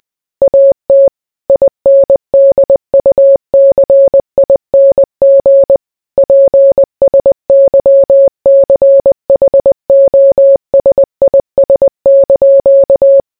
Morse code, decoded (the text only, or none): AT INDUCING PSYCHOSIS)